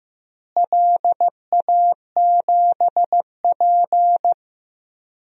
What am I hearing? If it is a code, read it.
Morse code, heard as a keyed tone, about 15 wpm, 700 Hz. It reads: LA7P
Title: DXCC-100 Call Signs - 4 Characters - Repeated 2x - 15wpm